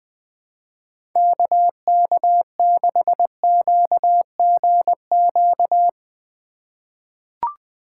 Morse code, heard as a keyed tone, about 20 wpm, 700 Hz. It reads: KK6QGQ
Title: USA Call Signs - 6 Characters - 2x Character Spacing - 20wpm